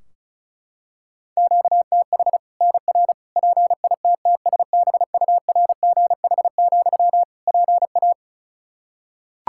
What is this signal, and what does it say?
Morse code, heard as a keyed tone, about 36 wpm, 700 Hz. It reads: QTH NR PITTSBURGH, PA
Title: Practice QSO - Exchange 1 - Beginning - 35wpm